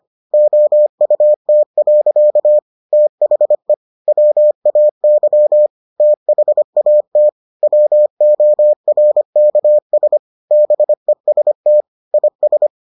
Morse code, heard as a keyed tone, about 25 wpm, 600 Hz. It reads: OUT. THE WAY THAT WORKS BEST IS